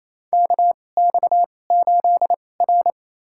Morse code, heard as a keyed tone, about 27 wpm, spaced as for 20 wpm, 700 Hz. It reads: KX8R
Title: USA Call Signs - 4 Characters - 2x Character Spacing - 28wpm